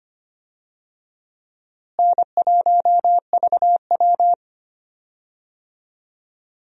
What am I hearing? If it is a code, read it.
Morse code, heard as a keyed tone, about 25 wpm, 700 Hz. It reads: N1VW
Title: USA Call Signs - 4 Characters - Repeated 2x - 25wpm